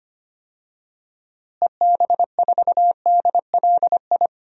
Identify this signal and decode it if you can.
Morse code, heard as a keyed tone, about 25 wpm, 700 Hz. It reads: EB4DLI
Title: DXCC-100 Call Signs - 6 Characters - Repeated 2x - 25wpm